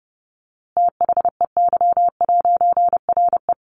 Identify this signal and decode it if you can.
Morse code, heard as a keyed tone, about 30 wpm, 700 Hz. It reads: THEY'RE
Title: A Sound Of Thunder 30wpm